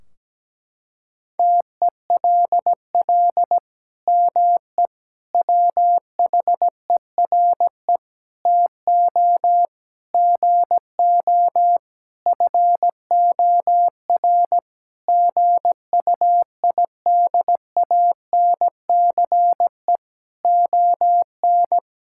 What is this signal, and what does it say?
Morse code, heard as a keyed tone, about 17 wpm, 700 Hz. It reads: TELL ME WHERE TO GO FOR GUIDANCE ON
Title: Sentences from Top 1600 Words 17wpm